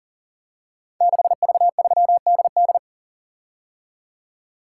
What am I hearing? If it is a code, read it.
Morse code, heard as a keyed tone, about 38 wpm, 700 Hz. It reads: 6V3DD
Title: Code Groups - 5 Letter:Numbers - 40wpm